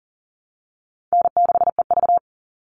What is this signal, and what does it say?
Morse code, heard as a keyed tone, about 40 wpm, 700 Hz. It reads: N6EV
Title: Easy Call Signs Repeated 3x - 40wpm